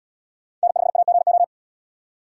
Morse code, heard as a keyed tone, about 77 wpm, 700 Hz. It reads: N5AGO